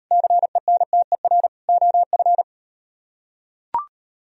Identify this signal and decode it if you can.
Morse code, heard as a keyed tone, about 38 wpm, 700 Hz. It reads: CENTER OF